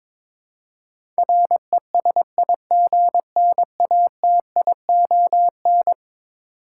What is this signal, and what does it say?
Morse code, heard as a keyed tone, about 21 wpm, 700 Hz. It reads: RESIGNATION
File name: practice / 22wpm / Eleven Letter Words - 22wpm